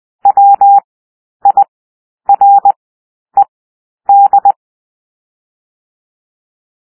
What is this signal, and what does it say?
Morse code, heard as a keyed tone, about 20 wpm, 800 Hz. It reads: W I R E D
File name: cwmp3 / W1 SOUND WORDS 20 wpm